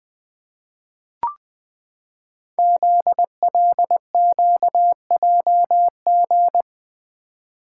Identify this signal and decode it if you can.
Morse code, heard as a keyed tone, about 20 wpm, 700 Hz. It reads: ZLQJG
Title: Code Groups - 5 Letters - 20wpm